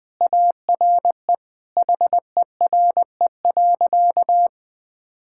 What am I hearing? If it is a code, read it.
Morse code, heard as a keyed tone, about 21 wpm, 700 Hz. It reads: ARE HERE.
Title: Sentences with 3-Word Phrases - Spoken Beginning - 20wpm